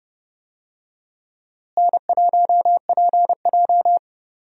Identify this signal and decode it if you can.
Morse code, heard as a keyed tone, about 30 wpm, 700 Hz. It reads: N1PJ
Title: USA Call Signs - 4 Characters - Repeated 2x - 30wpm